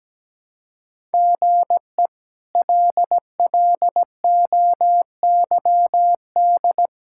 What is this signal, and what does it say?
Morse code, heard as a keyed tone, about 17 wpm, 700 Hz. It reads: GE LLOYD
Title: K1USN SST QSO Practice - 17wpm